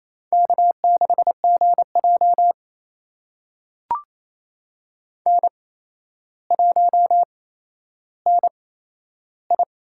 Morse code, heard as a keyed tone, about 28 wpm, spaced as for 12 wpm, 700 Hz. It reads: K6GJ N1NI
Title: USA Call Signs - 4 Characters - 8x Character Spacing - 28wpm